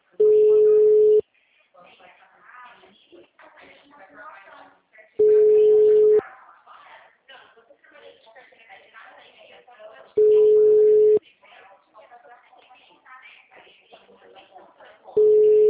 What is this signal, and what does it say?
Ringback tone (Europe, 425 Hz), with speech.
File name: out- -2012- - - .90